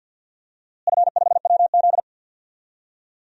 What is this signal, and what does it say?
Morse code, heard as a keyed tone, about 49 wpm, 700 Hz. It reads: F5WZ
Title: DXCC-100 Call Signs - 4 Characters - Repeated 3x - 50wpm